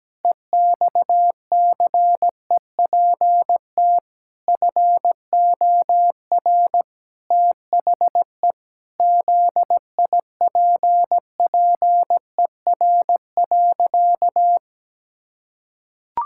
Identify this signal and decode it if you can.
Morse code, heard as a keyed tone, about 17 wpm, 700 Hz. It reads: EXCEPT FOR THE ZIPPER.
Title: Pangram Challenge v2 - 17wpm